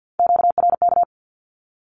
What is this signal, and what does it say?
Morse code, heard as a keyed tone, about 50 wpm, 700 Hz. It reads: QRK
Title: Q-Codes with Spoken Meaning - 50wpm